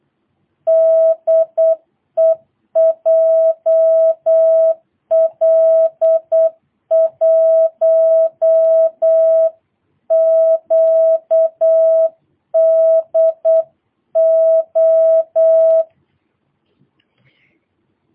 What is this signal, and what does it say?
Morse code, heard as a keyed tone, about 8 wpm, 650 Hz. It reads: DE JL1QDO
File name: jh8cbh / 2022cw